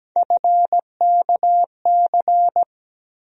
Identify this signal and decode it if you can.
Morse code, heard as a keyed tone, about 17 wpm, 700 Hz. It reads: FKC
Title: USA Call Sign Postfixes - 3 Letters 17wpm